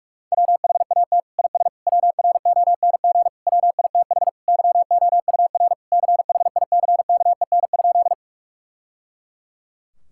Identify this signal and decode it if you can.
Morse code, heard as a keyed tone, about 43 wpm, 700 Hz. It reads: WHAT IS WRONG WITH YOUR CHICKEN?
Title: Sentences from Top 1000 Words - Review Part 5 - 45wpm